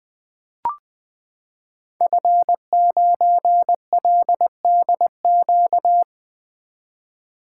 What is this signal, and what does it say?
Morse code, heard as a keyed tone, about 20 wpm, 700 Hz. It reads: F9LDQ
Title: Code Groups - 5 Letter:Numbers - 20wpm